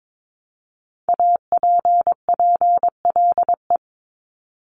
Morse code, heard as a keyed tone, about 22 wpm, 700 Hz. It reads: APPLE
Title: Multi-letter Phonemes Repeated 3x 22wpm